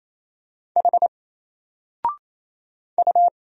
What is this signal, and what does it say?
Morse code, heard as a keyed tone, about 27 wpm, 700 Hz. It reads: H U
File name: Single-Letter-Number-28wpm